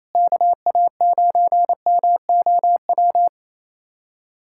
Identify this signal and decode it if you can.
Morse code, heard as a keyed tone, about 27 wpm, 700 Hz. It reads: KA9MOW
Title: USA Callsign DE Callsign - 6 Characters - 28wpm